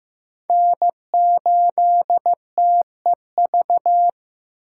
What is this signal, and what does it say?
Morse code, heard as a keyed tone, about 15 wpm, 700 Hz. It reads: N8TEV
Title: USA Call Signs - 3x Character Spacing - 15wpm